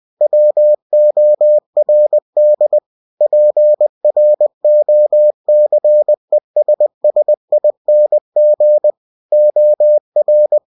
Morse code, heard as a keyed tone, about 20 wpm, 600 Hz. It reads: WORD PROCESSING OR
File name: cw-books / winnie-the-pooh / winnie-20wpm-page-0116